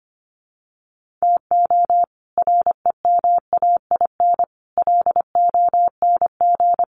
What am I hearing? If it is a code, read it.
Morse code, heard as a keyed tone, about 25 wpm, 700 Hz. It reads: TO REMAIN LONG
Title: Sentences from Top 1500 Words 25wpm